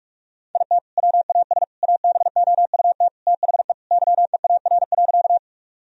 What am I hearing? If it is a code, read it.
Morse code, heard as a keyed tone, about 41 wpm, 700 Hz. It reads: IT WAS ABOUT THE YEAR.
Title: Sentences from Top 1000 Words - Review Part 1 - 45wpm